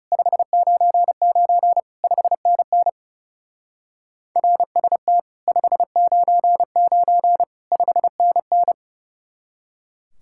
Morse code, heard as a keyed tone, about 37 wpm, 700 Hz. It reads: 599 5NN RST 599 5NN
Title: Speed-Racing Common QSO Phrases - 30wpm